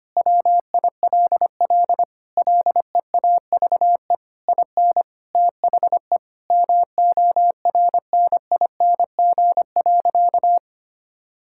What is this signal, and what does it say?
Morse code, heard as a keyed tone, about 25 wpm, 700 Hz. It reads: WILL LEAVE IN THE MORNING.
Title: Sentences from Top 1000 Words - Review Part 6 - 25wpm